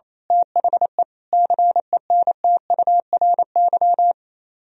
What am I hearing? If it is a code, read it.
Morse code, heard as a keyed tone, about 26 wpm, 700 Hz. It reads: THE CENTURY
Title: Speed-Racing - Sets of 5 Words 28wpm